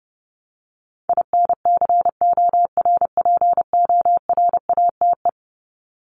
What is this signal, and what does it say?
Morse code, heard as a keyed tone, about 29 wpm, 700 Hz. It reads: INCORPORATE
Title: Top 1751-2000 Words 30wpm